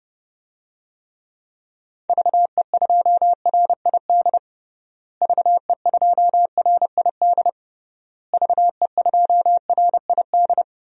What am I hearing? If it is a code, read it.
Morse code, heard as a keyed tone, about 32 wpm, 700 Hz. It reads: VE2RID VE2RID VE2RID
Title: DXCC-100 Call Signs - 6 Characters - Repeated 3x - 30wpm